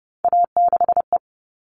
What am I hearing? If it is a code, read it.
Morse code, heard as a keyed tone, about 30 wpm, 700 Hz. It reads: A6E
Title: All Cut Numbers - 3-Digit 30wpm